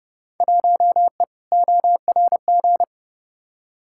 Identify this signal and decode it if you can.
Morse code, heard as a keyed tone, about 30 wpm, 700 Hz. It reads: 1E ORG
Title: Field Day Exchanges - 30wpm